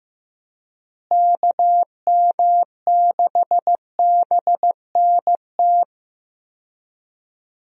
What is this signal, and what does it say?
Morse code, heard as a keyed tone, about 15 wpm, 700 Hz. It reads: KM6BNT
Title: USA Call Signs - 6 Characters - Repeated 2x - 15wpm